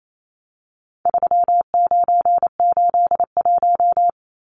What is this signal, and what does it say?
Morse code, heard as a keyed tone, about 28 wpm, 700 Hz. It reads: 3981
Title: Numbers - 4-Digit 28wpm